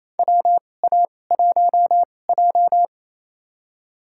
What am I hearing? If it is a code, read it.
Morse code, heard as a keyed tone, about 28 wpm, spaced as for 20 wpm, 700 Hz. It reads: WA1J
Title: USA Call Signs - 4 Characters - 2x Character Spacing - 28wpm